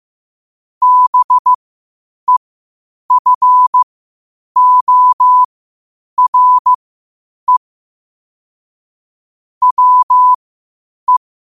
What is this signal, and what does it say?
Morse code, heard as a keyed tone, about 15 wpm, spaced as for 8 wpm, 1000 Hz. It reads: BEFORE WE
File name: qotdfiles / 8wpm